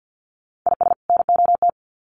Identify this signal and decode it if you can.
Morse code, heard as a keyed tone, about 50 wpm, 700 Hz. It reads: IS NOT
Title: Sets of 2 Words 50wpm